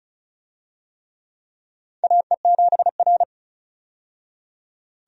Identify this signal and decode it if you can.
Morse code, heard as a keyed tone, about 34 wpm, 700 Hz. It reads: AE7R
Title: USA Call Signs - 4 Characters - 35wpm